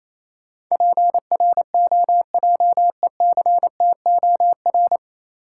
Morse code, heard as a keyed tone, about 28 wpm, 690 Hz. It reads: PROJECTOR